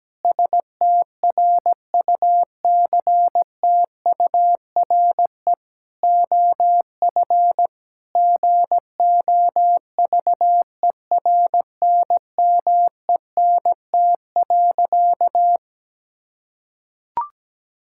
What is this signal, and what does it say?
Morse code, heard as a keyed tone, about 17 wpm, 700 Hz. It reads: STRUCTURE OF GOVERNMENT.
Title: Sentences from Top 1000 Words - Review Part 6 - 17wpm